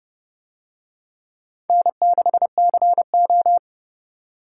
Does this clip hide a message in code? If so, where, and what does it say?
Yes: Morse code from 1.7 to 3.6 s, heard as a keyed tone, about 27 wpm, 700 Hz. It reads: N6CO